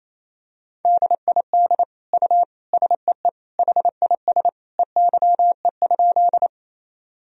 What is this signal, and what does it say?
Morse code, heard as a keyed tone, about 28 wpm, 700 Hz. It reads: DID U SEE HIS EYE?
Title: Sentences from Top 300 Words and Abbreviations 28wpm